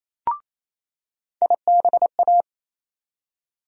Morse code, heard as a keyed tone, about 28 wpm, 700 Hz. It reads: IBA
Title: USA Call Sign Postfixes - 3 Letters 28wpm